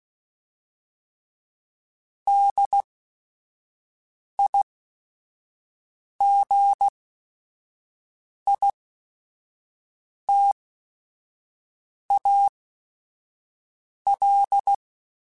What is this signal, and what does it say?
Morse code, heard as a keyed tone, about 16 wpm, spaced as for 5 wpm, 800 Hz. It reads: DIGITAL
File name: Technical Subjects